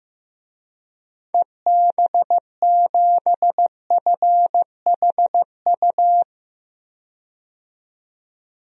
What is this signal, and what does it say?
Morse code, heard as a keyed tone, about 15 wpm, 700 Hz. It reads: EB7FHU
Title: DXCC-100 Call Signs - 6 Characters - 15wpm